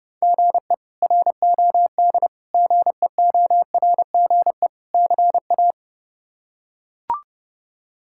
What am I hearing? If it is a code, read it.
Morse code, heard as a keyed tone, about 30 wpm, 700 Hz. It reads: GE ROD GEORGE CA
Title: K1USN SST QSO Practice - 30wpm